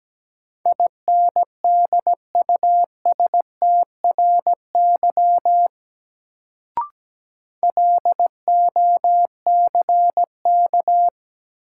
Morse code, heard as a keyed tone, about 17 wpm, 700 Hz. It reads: INDUSTRY LOCK
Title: Sentences from Top 1300 Words - New Word Drill Alt - 17wpm